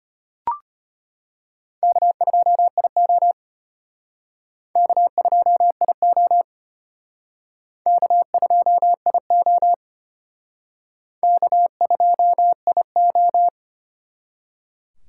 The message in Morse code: K2IO K2IO K2IO K2IO

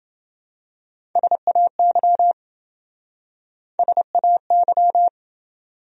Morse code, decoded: SAY SAY

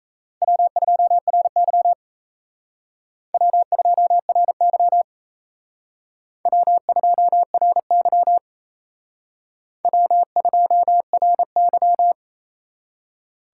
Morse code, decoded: W2RY W2RY W2RY W2RY

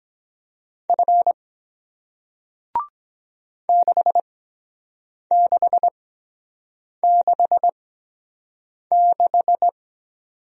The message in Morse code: F 6666

